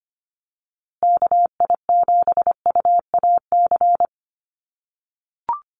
Morse code, heard as a keyed tone, about 25 wpm, 700 Hz. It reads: KI7UAC
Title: USA Call Signs Repeated 3x 25wpm